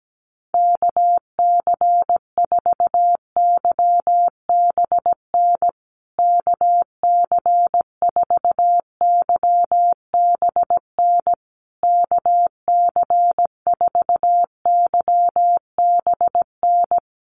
Morse code, heard as a keyed tone, about 17 wpm, 700 Hz. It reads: KC4YBN KC4YBN KC4YBN
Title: USA Call Signs Repeated 3x 17wpm